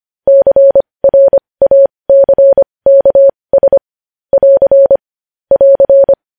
Morse code, heard as a keyed tone, about 25 wpm, spaced as for 20 wpm, 550 Hz. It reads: CRACKS <AR> <AR>
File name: podcast / News headlines-1x-25x20